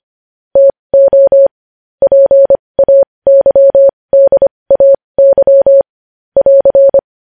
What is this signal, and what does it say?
Morse code, heard as a keyed tone, about 25 wpm, spaced as for 20 wpm, 550 Hz. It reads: TO PAYDAY <AR>
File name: podcast / News headlines-1x-25x20